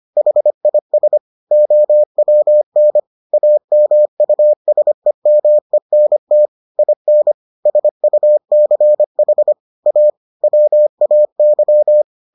Morse code, heard as a keyed tone, about 25 wpm, 600 Hz. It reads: HIS OWN AMUSEMENT IN SUCH A WAY